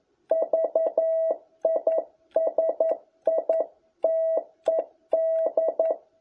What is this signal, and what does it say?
Morse code, heard as a keyed tone, about 11 wpm, 650 Hz. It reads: VISITED
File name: jh8cbh / 2021cw